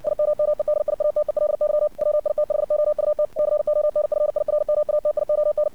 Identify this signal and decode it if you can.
Morse code, heard as a keyed tone, about 64 wpm, 600 Hz. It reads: IMMEDIATELY WITHOUT CONFIRMATION
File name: archived / CRSnet